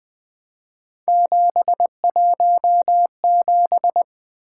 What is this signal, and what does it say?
Morse code, heard as a keyed tone, about 23 wpm, 700 Hz. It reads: 717
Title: Speed-Racing - Three Letter-Number Combinations - 20wpm